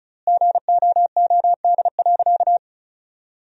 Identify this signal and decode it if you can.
Morse code, heard as a keyed tone, about 36 wpm, 700 Hz. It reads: GOOD.
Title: 4x Word-Spacing - Sentences from Top 200 Words - 35wpm